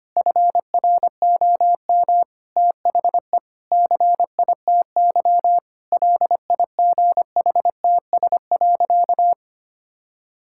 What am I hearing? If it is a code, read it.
Morse code, heard as a keyed tone, about 25 wpm, 700 Hz. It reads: FROM THE CITY LIGHTS.
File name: practice / 25wpm / Difficult Sentences with Hints - 25wpm